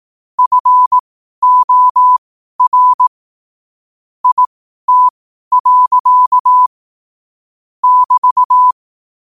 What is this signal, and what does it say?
Morse code, heard as a keyed tone, about 18 wpm, spaced as for 12 wpm, 1000 Hz. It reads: FOR IT. <BT>
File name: qotdfiles / 12wpm